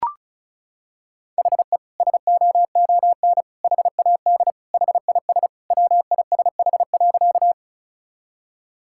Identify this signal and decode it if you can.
Morse code, heard as a keyed tone, about 36 wpm, 700 Hz. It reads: HE SOON HAD HIS WISH.
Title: Sentences from Top 1600 Words 35wpm